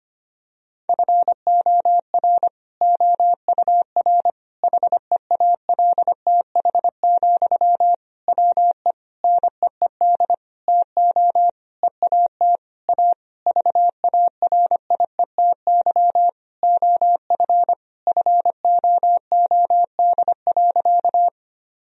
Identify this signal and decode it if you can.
Morse code, heard as a keyed tone, about 25 wpm, 700 Hz. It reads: FOR OUR HEALTH, WE NEED TO EAT A VARIETY OF FOOD.
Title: Sentences from Top 1000 Words - Review Part 4 - 25wpm